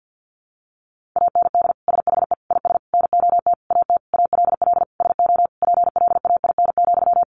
Morse code, heard as a keyed tone, about 50 wpm, 700 Hz. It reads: AND SHE IS NOT AT ALL SO PLAIN,
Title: Sentences from Top 600 Words 50wpm